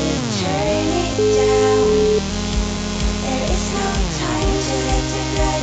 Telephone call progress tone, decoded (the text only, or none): ringback tone